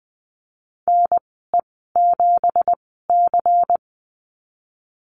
Morse code, heard as a keyed tone, about 19 wpm, spaced as for 14 wpm, 700 Hz. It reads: NE7C
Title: 173 Lesson 28g - Number 9 - Call Signs 20wpm